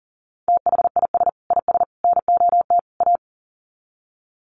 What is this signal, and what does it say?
Morse code, heard as a keyed tone, about 40 wpm, 700 Hz. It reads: THIS IS NOT A
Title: Sets of 4 Words 40wpm